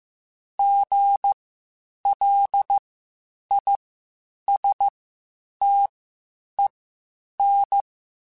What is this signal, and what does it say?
Morse code, heard as a keyed tone, about 15 wpm, spaced as for 8 wpm, 800 Hz. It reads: GLISTEN
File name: Spaced Alphas / Spaced Alphas mp3s / Gs sp @13